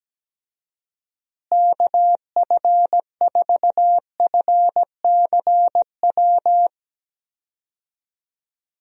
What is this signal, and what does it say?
Morse code, heard as a keyed tone, about 17 wpm, 700 Hz. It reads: KF4FCW
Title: USA Call Signs - 6 Characters - Repeated 3x - 17wpm